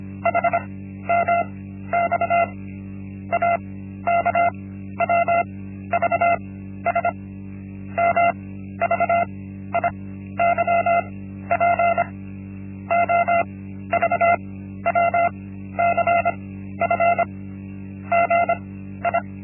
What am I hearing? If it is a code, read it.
Morse code, heard as a keyed tone, about 25 wpm, 690 Hz. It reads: H M X A K W V S M V I Y P O V W C F G I
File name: Linksmakalnis-1